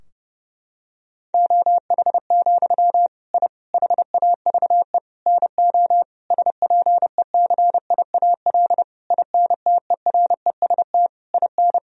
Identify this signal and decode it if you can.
Morse code, heard as a keyed tone, about 32 wpm, 700 Hz. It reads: OH, I HAVE NO SPECIAL INTEREST IN
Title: Sentences from Top 1000 Words - Review Part 3 - 30wpm